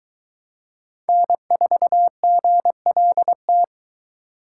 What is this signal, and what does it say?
Morse code, heard as a keyed tone, about 22 wpm, 700 Hz. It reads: N4GLT